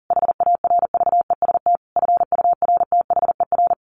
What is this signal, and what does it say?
Morse code, heard as a keyed tone, about 40 wpm, 700 Hz. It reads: HARVEST FURTHER
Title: A Sound Of Thunder 40wpm